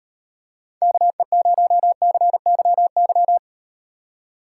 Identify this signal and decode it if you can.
Morse code, heard as a keyed tone, about 37 wpm, 700 Hz. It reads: KE0CYY